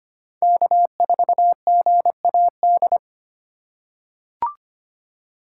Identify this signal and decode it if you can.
Morse code, heard as a keyed tone, about 25 wpm, 700 Hz. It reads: K4GAD
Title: USA Call Signs - 5 Characters - 2x Character Spacing - 25wpm